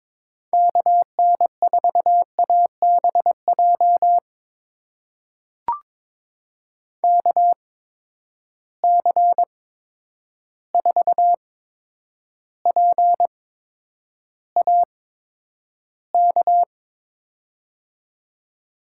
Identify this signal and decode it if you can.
Morse code, heard as a keyed tone, about 22 wpm, spaced as for 9 wpm, 700 Hz. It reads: KN4ABJ KC4PAK
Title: USA Call Signs - 6 Characters - 8x Character Spacing - 22wpm